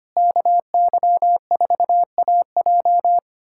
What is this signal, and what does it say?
Morse code, heard as a keyed tone, about 25 wpm, 700 Hz. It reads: KY4AJ
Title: USA Callsign DE Callsign - 5 Characters - 25wpm